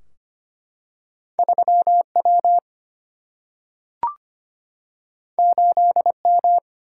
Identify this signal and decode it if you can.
Morse code, heard as a keyed tone, about 25 wpm, 700 Hz. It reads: 3 W 8 M
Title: Single-Letter Single-Number 25wpm